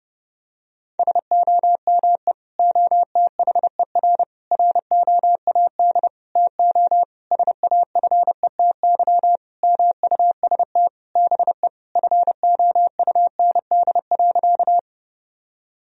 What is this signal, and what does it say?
Morse code, heard as a keyed tone, about 30 wpm, 700 Hz. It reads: SOME OTHER ROAD TO SAFETY MUST BE FOUND.
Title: Sentences from Top 1000 Words - Review Part 4 - 30wpm